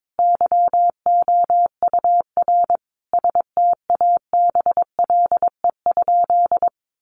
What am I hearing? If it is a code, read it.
Morse code, heard as a keyed tone, about 22 wpm, 700 Hz. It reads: YOUR STABLE?
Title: Sentences from Top 1500 Words 22wpm